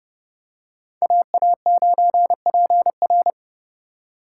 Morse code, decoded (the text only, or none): AA9PR